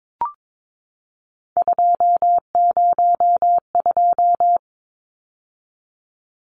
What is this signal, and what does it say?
Morse code, heard as a keyed tone, about 22 wpm, 700 Hz. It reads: 202
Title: Numbers - 3-Digit 22wpm